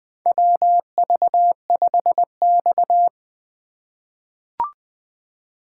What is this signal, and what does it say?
Morse code, heard as a keyed tone, about 20 wpm, 700 Hz. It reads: WV5X
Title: USA Call Signs - 4 Characters - 2x Character Spacing - 20wpm